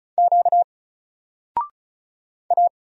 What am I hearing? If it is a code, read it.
Morse code, heard as a keyed tone, about 36 wpm, 700 Hz. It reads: Q A
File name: Single-Letter-Number-35wpm